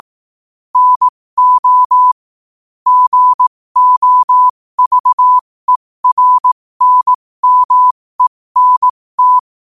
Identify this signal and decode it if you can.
Morse code, heard as a keyed tone, about 18 wpm, spaced as for 15 wpm, 1000 Hz. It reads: NO GOVERNMENT
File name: qotdfiles / 15wpm